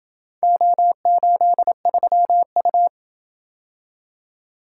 Morse code, heard as a keyed tone, about 28 wpm, 700 Hz. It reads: O83U